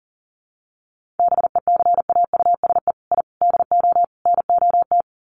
Morse code, heard as a keyed tone, about 40 wpm, 700 Hz. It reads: BECAUSE I DO NOT
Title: Sets of 4 Words 40wpm